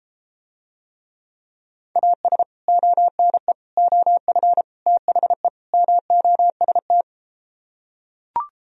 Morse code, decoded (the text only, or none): AS ONE OF THE MOST